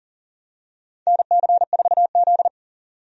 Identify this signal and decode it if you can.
Morse code, heard as a keyed tone, about 38 wpm, 700 Hz. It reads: NC4Z
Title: USA Call Signs - 4 Characters - 40wpm